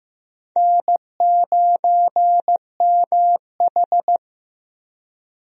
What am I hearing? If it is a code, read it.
Morse code, heard as a keyed tone, about 15 wpm, 700 Hz. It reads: N9MH
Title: USA Call Signs - 4 Characters - Repeated 3x - 15wpm